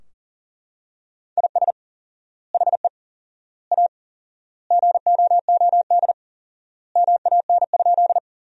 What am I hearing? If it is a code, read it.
Morse code, heard as a keyed tone, about 40 wpm, 700 Hz. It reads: IS HE A GOOD MAN?
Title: 4x Word-Spacing - Sentences from Top 200 Words - 40wpm